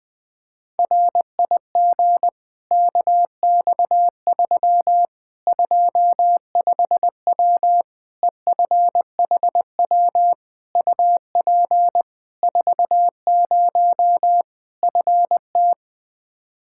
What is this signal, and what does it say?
Morse code, heard as a keyed tone, about 20 wpm, 700 Hz. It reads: RIG KX3 25W EFHW UP 40 FT
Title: Practice QSO - Exchange 2 - Info - 20wpm